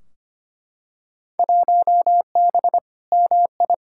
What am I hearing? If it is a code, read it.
Morse code, heard as a keyed tone, about 25 wpm, 700 Hz. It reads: 1B MI
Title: Field Day Exchanges - 25wpm